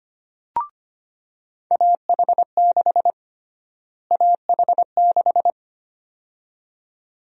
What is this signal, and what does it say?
Morse code, heard as a keyed tone, about 25 wpm, 700 Hz. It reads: AH6 AH6
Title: USA Call Sign Prefixes - Repeated 2x - 25wpm